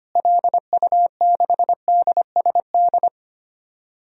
Morse code, heard as a keyed tone, about 25 wpm, 690 Hz. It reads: LU6DSD
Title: DXCC-100 Call Signs - 6 Characters - Repeated 3x - 25wpm